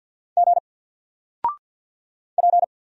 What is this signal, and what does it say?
Morse code, heard as a keyed tone, about 49 wpm, 700 Hz. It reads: G P
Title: Single Character (Letter:Number:Punctuation:Prosign) 50wpm